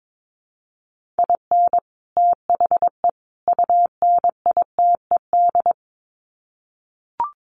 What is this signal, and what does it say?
Morse code, heard as a keyed tone, about 22 wpm, 700 Hz. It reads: IN THE UNITED
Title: Sets of 3 Words 22wpm